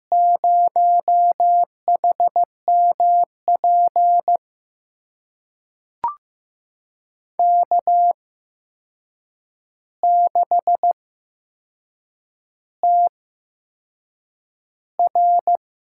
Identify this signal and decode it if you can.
Morse code, heard as a keyed tone, about 15 wpm, spaced as for 6 wpm, 700 Hz. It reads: 0HMP K6TR
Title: USA Call Signs - 8x Character Spacing - 15wpm